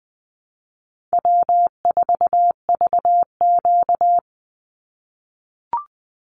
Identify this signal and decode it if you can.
Morse code, heard as a keyed tone, about 20 wpm, 700 Hz. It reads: W4VQ
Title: Easy Call Signs - 20wpm